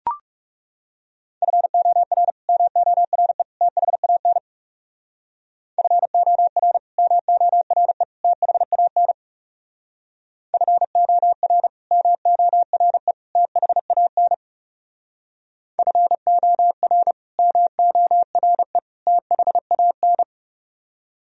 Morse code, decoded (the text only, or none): FOR MORE THAN FOR MORE THAN FOR MORE THAN FOR MORE THAN